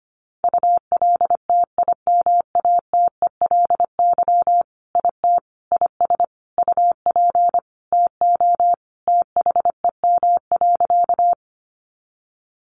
Morse code, heard as a keyed tone, about 25 wpm, 700 Hz. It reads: ULTIMATELY IT IS UP TO THEM.
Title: Sentences from Top 1200 Words 25wpm